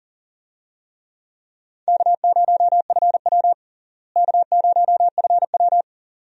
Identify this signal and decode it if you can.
Morse code, heard as a keyed tone, about 40 wpm, 700 Hz. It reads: K0FW K0FW
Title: USA Call Signs - 4 Characters - Repeated 2x - 40wpm